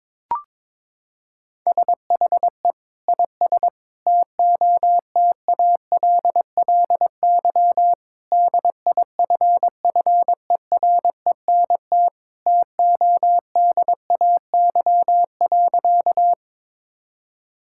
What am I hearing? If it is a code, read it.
Morse code, heard as a keyed tone, about 22 wpm, 700 Hz. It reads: SHE IS TOTALLY DIFFERENT TODAY.
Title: Sentences from Top 1600 Words 22wpm